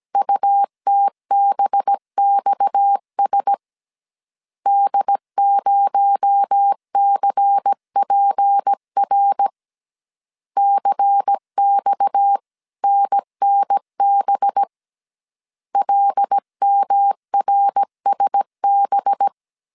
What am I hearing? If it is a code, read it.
Morse code, heard as a keyed tone, about 17 wpm, 800 Hz. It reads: UTBXS D0CPR CXNNB LMRSB
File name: CW / DET / 1 6 1500C S A 16wpm 2635